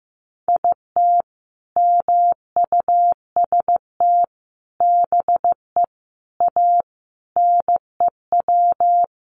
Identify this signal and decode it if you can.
Morse code, heard as a keyed tone, about 15 wpm, 700 Hz. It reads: IT MUST BE A NEW
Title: Sentences from Top 500 Words 15wpm